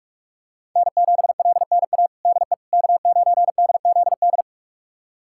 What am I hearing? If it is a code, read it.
Morse code, heard as a keyed tone, about 46 wpm, 700 Hz. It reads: N7LNA DE K9DZD
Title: USA Callsign DE Callsign - 5 Characters - 45wpm